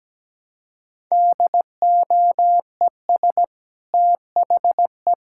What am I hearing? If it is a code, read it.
Morse code, heard as a keyed tone, about 17 wpm, 700 Hz. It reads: DOES THE
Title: Sentences from Top 1000 Words - Review Part 5 - 17wpm